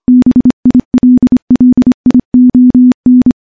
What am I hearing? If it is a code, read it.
Morse code, heard as a keyed tone, about 25 wpm, 250 Hz. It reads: BILLION
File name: morse-training-25WPM